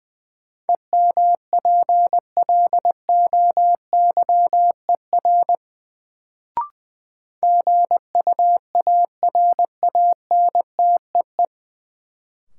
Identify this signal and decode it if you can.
Morse code, heard as a keyed tone, about 20 wpm, 700 Hz. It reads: EMPLOYER GUARANTEE
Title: Sentences from Top 1600 Words - New Word Drill - 20wpm